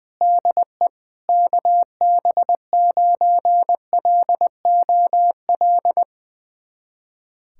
Morse code, heard as a keyed tone, about 20 wpm, 700 Hz. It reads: DE KB9LOL
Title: USA Callsign DE Callsign - 6 Characters - 20wpm